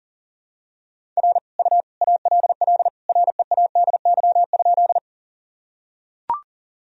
Morse code, heard as a keyed tone, about 40 wpm, 700 Hz. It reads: R U ALL READY?
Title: Sentences from Top 400 Words and Abbreviations 40wpm